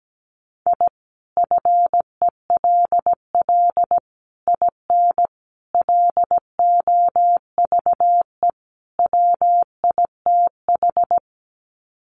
Morse code, heard as a keyed tone, about 17 wpm, 700 Hz. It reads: I FELL IN LOVE WITH
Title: Sets of 5 Words Encore 17wpm